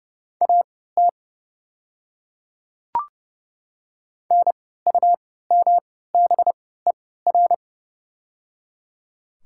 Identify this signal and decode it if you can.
Morse code, heard as a keyed tone, about 30 wpm, spaced as for 17 wpm, 700 Hz. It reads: AT NUMBER
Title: 3x Character Spacing - Top 100 Words - 30wpm